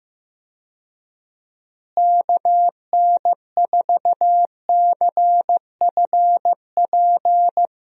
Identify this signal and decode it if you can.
Morse code, heard as a keyed tone, about 16 wpm, 700 Hz. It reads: KN4CFP
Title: USA Call Signs - 6 Characters - Repeated 3x - 15wpm